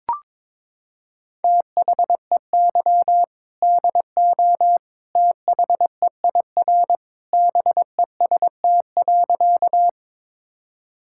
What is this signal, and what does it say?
Morse code, heard as a keyed tone, about 21 wpm, 700 Hz. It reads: THEY DO THEIR BEST.
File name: practice / 22wpm / Sentences with 2-Word Phrases - 22wpm